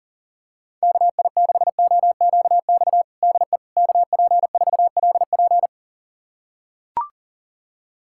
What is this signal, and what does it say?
Morse code, heard as a keyed tone, about 40 wpm, 700 Hz. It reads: KI6OQX DE KP4LP
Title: USA Callsign DE Callsign 40wpm